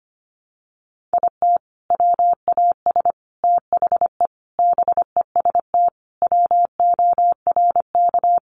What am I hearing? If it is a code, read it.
Morse code, heard as a keyed tone, about 25 wpm, 700 Hz. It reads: IT WAS THE BEST WORK
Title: Sentences from Top 1300 Words 25wpm